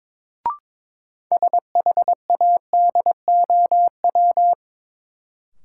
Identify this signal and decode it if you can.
Morse code, heard as a keyed tone, about 22 wpm, 700 Hz. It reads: SHADOW
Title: Sets of 5 Words Encore - New Word Drill - 22wpm